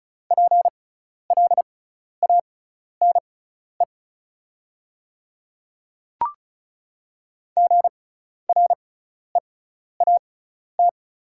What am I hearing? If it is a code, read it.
Morse code, heard as a keyed tone, about 34 wpm, spaced as for 12 wpm, 690 Hz. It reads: PLANE GREAT
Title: 6x Character Spacing - Top 500 Words - 35wpm